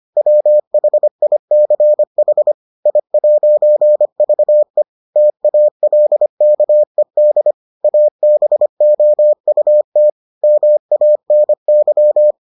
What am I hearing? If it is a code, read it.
Morse code, heard as a keyed tone, about 25 wpm, 600 Hz. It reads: WHICH I'VE TALKED ABOUT MANY